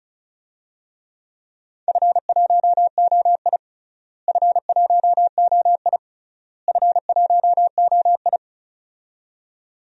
Morse code, heard as a keyed tone, about 34 wpm, 700 Hz. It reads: F1OI F1OI F1OI
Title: DXCC-100 Call Signs - 4 Characters - Repeated 3x - 35wpm